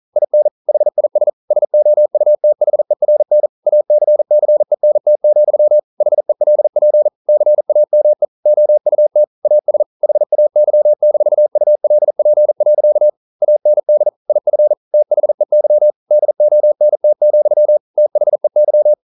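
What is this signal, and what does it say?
Morse code, heard as a keyed tone, about 42 wpm, 600 Hz. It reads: IN HIS SOUTHERN ACCENT, HELP CAME OUT AS HAY-ULP. AND IF THEY DONT, THEY
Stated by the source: Morse code practice at 41 wpm, from CW med Gnister.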